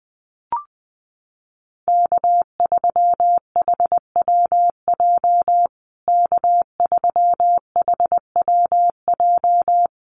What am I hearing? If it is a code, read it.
Morse code, heard as a keyed tone, about 20 wpm, 700 Hz. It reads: K3HWJ K3HWJ
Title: USA Call Signs Repeated 3x 20wpm